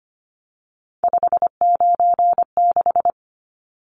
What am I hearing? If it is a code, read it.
Morse code, heard as a keyed tone, about 25 wpm, 690 Hz. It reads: 596
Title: Numbers - 3-Digit 25wpm